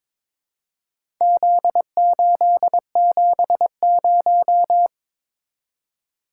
Morse code, heard as a keyed tone, about 22 wpm, 700 Hz. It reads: Z870
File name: practice / 22wpm / Four Letter-Number Combinations - 22wpm